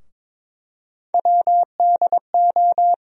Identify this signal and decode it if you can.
Morse code, heard as a keyed tone, about 22 wpm, 700 Hz. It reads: WDO
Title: USA Call Sign Postfixes - 3 Letters 22wpm